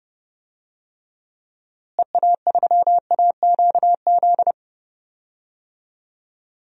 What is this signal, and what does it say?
Morse code, heard as a keyed tone, about 30 wpm, 700 Hz. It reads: EA3AQZ
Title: DXCC-100 Call Signs - 6 Characters - Repeated 2x - 30wpm